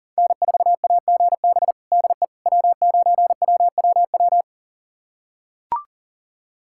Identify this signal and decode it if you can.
Morse code, heard as a keyed tone, about 37 wpm, 700 Hz. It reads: N4AGB DE W9WWW
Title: USA Callsign DE Callsign - 5 Characters - 40wpm